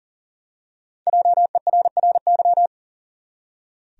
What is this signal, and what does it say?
Morse code, heard as a keyed tone, about 40 wpm, 700 Hz. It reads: JERRY
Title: USA Common Names - 40wpm